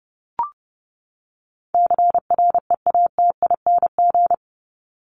Morse code, heard as a keyed tone, about 30 wpm, 700 Hz. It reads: CREATING
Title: ING Suffix - Top 100 Words with Base Words 30wpm